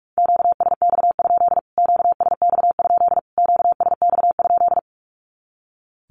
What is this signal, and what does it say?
Morse code, heard as a keyed tone, about 44 wpm, 700 Hz. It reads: QSX? QSX? QSX?
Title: Q-Codes Repeated 3x 45wpm